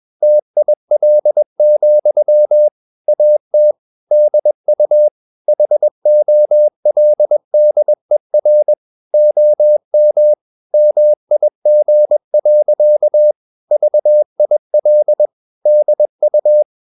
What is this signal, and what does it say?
Morse code, heard as a keyed tone, about 21 wpm, 600 Hz. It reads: TIL, AT DU HOLDER OM MIG. VIL DU
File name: Ferie QRS 21wpm